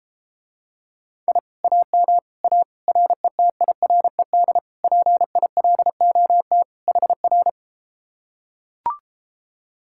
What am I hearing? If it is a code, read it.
Morse code, heard as a keyed tone, about 33 wpm, 700 Hz. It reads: I AM A RETIRED PILOT HR